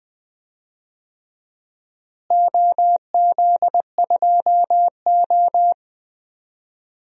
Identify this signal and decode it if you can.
Morse code, heard as a keyed tone, about 20 wpm, 700 Hz. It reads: OZ2O
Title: DXCC-100 Call Signs - 4 Characters - Repeated 2x - 20wpm